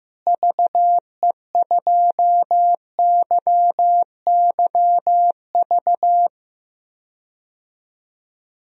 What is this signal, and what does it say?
Morse code, heard as a keyed tone, about 15 wpm, 700 Hz. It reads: VE2YYV
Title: DXCC-100 Call Signs - 6 Characters - Repeated 2x - 15wpm